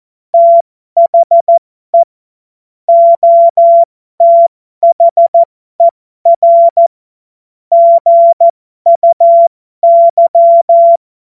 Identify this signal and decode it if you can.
Morse code, heard as a keyed tone, about 14 wpm, spaced as for 12 wpm, 680 Hz. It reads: THE OTHER GUY
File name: Morsekurs / Witze